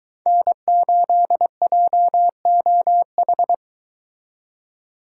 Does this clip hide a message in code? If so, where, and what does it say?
Yes: Morse code from 0.3 to 3.5 s, heard as a keyed tone, about 22 wpm, 700 Hz. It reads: N8JOH